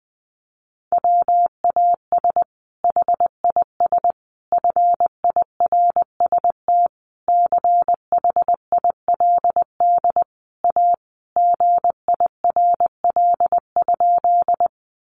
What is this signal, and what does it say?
Morse code, heard as a keyed tone, about 20 wpm, 700 Hz. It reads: WAS HIS FIRST CHILD A GIRL?
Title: Sentences from Top 800 Words 20wpm